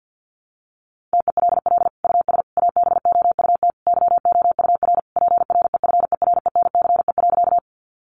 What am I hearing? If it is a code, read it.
Morse code, heard as a keyed tone, about 50 wpm, 700 Hz. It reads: TELL US ABOUT YOUR PREFERENCE.
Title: Sentences from Top 1500 Words 50wpm